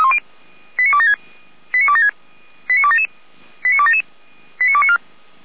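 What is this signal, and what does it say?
Five-tone selective call (ZVEI), heard as quick runs of five tones, tones of about 70 ms. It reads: 89276 89276 89277 89277 89294
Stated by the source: ZVEI five-tone selective call received via SDR.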